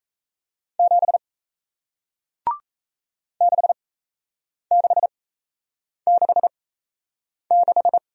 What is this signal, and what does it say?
Morse code, heard as a keyed tone, about 38 wpm, spaced as for 8 wpm, 700 Hz. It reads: 7 6666